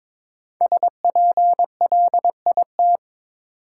Morse code, heard as a keyed tone, about 22 wpm, 690 Hz. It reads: SPLIT
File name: 5 Letter Words - First 3 Letters Given - 22wpm